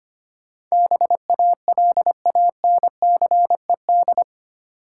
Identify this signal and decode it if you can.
Morse code, heard as a keyed tone, about 25 wpm, 690 Hz. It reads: BALANCED
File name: 8 Letter Words - First 3 Letters Given - 25wpm